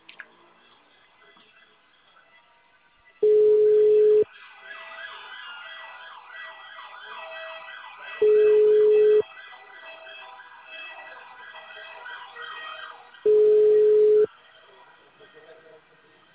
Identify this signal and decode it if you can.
Ringback tone (Europe, 425 Hz), with speech.